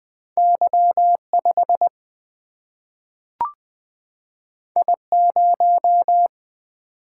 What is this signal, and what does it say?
Morse code, heard as a keyed tone, about 20 wpm, 700 Hz. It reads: Y 5 I 0
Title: Single-Letter Single-Number 20wpm